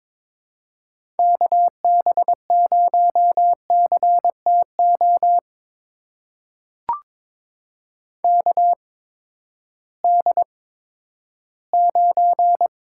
Morse code, heard as a keyed tone, about 22 wpm, spaced as for 13 wpm, 700 Hz. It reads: KB0CTO KD9
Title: USA Call Signs - 6 Characters - 8x Character Spacing - 22wpm